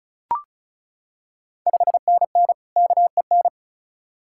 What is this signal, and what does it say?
Morse code, heard as a keyed tone, about 36 wpm, 700 Hz. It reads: 5NN KEN
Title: Ontario QSO Party Exchange 35wpm